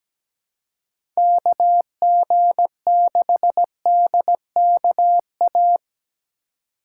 Morse code, heard as a keyed tone, about 17 wpm, 700 Hz. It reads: KG6DKA
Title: USA Call Signs - 4x Character Spacing - 17wpm